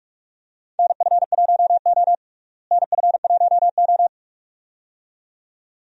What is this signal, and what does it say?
Morse code, heard as a keyed tone, about 46 wpm, 700 Hz. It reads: NF1O NF1O
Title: USA Call Signs - 4 Characters - Repeated 2x - 45wpm